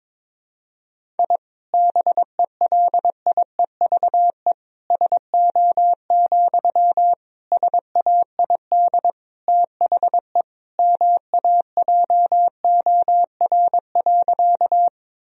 Morse code, heard as a keyed tone, about 22 wpm, 700 Hz. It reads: I BELIEVE SO, SAID THE MAJOR.
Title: Sentences from Top 1000 Words - Review Part 4 - 22wpm